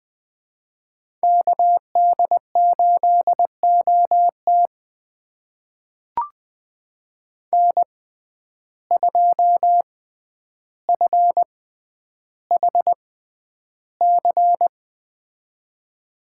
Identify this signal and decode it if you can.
Morse code, heard as a keyed tone, about 20 wpm, spaced as for 10 wpm, 700 Hz. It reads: KD8OT N2FHC
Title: USA Call Signs - 5 Characters - 6x Character Spacing - 20wpm